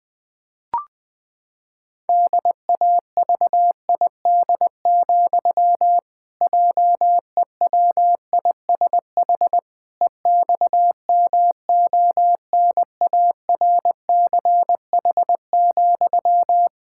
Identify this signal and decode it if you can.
Morse code, heard as a keyed tone, about 20 wpm, 700 Hz. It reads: DAVID, JEWISH EXMONARCH,
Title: Pangram Challenge v2 - 20wpm